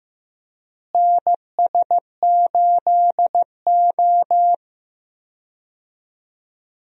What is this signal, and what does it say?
Morse code, heard as a keyed tone, about 15 wpm, 700 Hz. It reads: NS8O
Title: USA Call Signs - 4 Characters - Repeated 2x - 15wpm